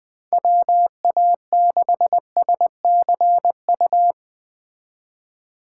Morse code, heard as a keyed tone, about 20 wpm, 700 Hz. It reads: WA6SCU
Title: Speed Racing - USA 2x3 Call Signs 17wpm